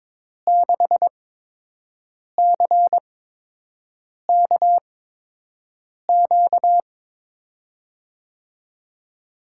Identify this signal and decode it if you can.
Morse code, heard as a keyed tone, about 22 wpm, spaced as for 6 wpm, 700 Hz. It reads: 6CKQ
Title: USA Call Signs - 6 Characters - 8x Character Spacing - 22wpm